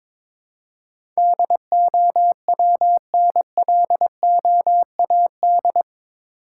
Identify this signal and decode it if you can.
Morse code, heard as a keyed tone, about 22 wpm, 700 Hz. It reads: DOWNLOAD
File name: practice / 22wpm / Eight Letter Words - 22wpm